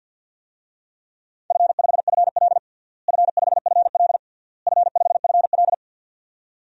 Morse code, heard as a keyed tone, about 53 wpm, 700 Hz. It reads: F5FL F5FL F5FL
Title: DXCC-100 Call Signs - 4 Characters - Repeated 3x - 50wpm